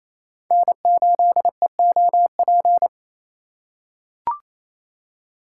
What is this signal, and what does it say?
Morse code, heard as a keyed tone, about 28 wpm, 700 Hz. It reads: N8EOP
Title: USA Call Signs - 5 Characters - 6x Character Spacing - 28wpm